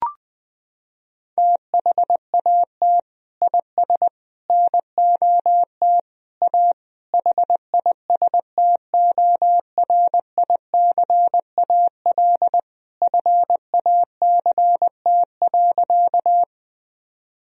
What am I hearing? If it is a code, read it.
Morse code, heard as a keyed tone, about 20 wpm, 700 Hz. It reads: THAT IS NOT A HISTORICAL FACT.
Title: Sentences from Top 1000 Words - Review Part 4 - 20wpm